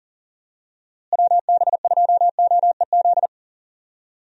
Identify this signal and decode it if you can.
Morse code, heard as a keyed tone, about 40 wpm, 700 Hz. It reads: WB2OEZ
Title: USA Call Signs - 6 Characters - Repeated 2x - 40wpm